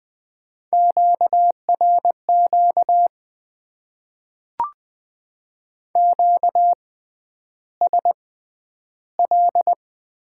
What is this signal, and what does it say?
Morse code, heard as a keyed tone, about 20 wpm, spaced as for 10 wpm, 700 Hz. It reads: QRQ QSL
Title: Top 90 QSO Elements - 6x Character Spacing - 20wpm